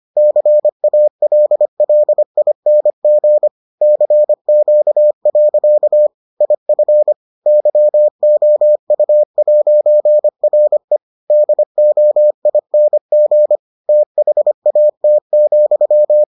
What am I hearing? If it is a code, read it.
Morse code, heard as a keyed tone, about 25 wpm, 600 Hz. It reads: CALLING CQ. IF YOU'RE DOING THAT,